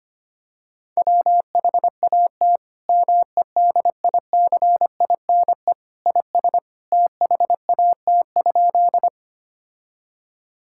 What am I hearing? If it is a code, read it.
Morse code, heard as a keyed tone, about 25 wpm, 700 Hz. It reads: WHAT MEDICINE IS THAT?
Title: Sentences from Top 1000 Words - Review Part 6 - 25wpm